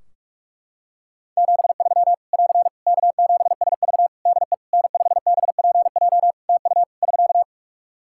Morse code, heard as a keyed tone, about 41 wpm, 700 Hz. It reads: 73 <AR> K7SV DE N5BPJ TU <SK>
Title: Practice QSO - Exchange 4 - Ending - 45wpm